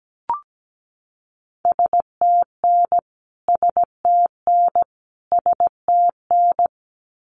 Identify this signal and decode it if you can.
Morse code, heard as a keyed tone, about 17 wpm, 700 Hz. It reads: STN STN STN
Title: Abbreviations Repeated 3x - 17wpm